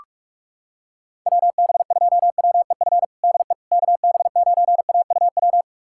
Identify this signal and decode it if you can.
Morse code, heard as a keyed tone, about 46 wpm, 700 Hz. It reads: WB2WEF DE KB9AUW
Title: USA Callsign DE Callsign 45wpm